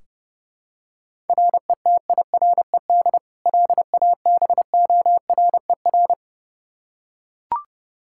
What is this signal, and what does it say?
Morse code, heard as a keyed tone, about 30 wpm, 700 Hz. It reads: RETIRED LABORER
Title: K1USN SST QSO Practice 30wpm